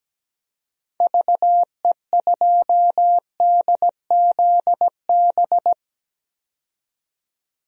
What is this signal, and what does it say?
Morse code, heard as a keyed tone, about 17 wpm, 700 Hz. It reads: VE2DZB
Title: DXCC-100 Call Signs - 6 Characters - 17wpm